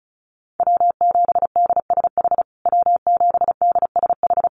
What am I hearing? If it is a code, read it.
Morse code, heard as a keyed tone, about 34 wpm, 700 Hz. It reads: W7DSH W7DSH
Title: USA Call Signs Repeated 3x 35wpm